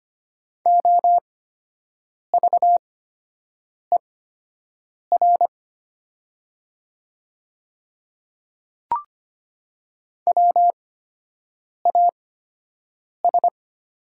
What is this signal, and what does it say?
Morse code, heard as a keyed tone, about 25 wpm, spaced as for 7 wpm, 700 Hz. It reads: OVER WAS
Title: 8x Character Spacing - Top 100 Words - 25wpm